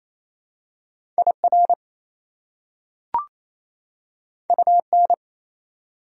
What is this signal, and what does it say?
Morse code, heard as a keyed tone, about 28 wpm, 700 Hz. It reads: IR UN
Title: Prefixes and Suffixes 28wpm